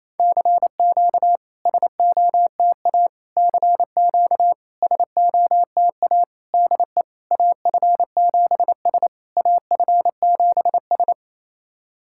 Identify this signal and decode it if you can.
Morse code, heard as a keyed tone, about 28 wpm, 700 Hz. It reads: CQ SOTA CQ SOTA DE AF7S AF7S
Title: SOTA QSO Practice 28wpm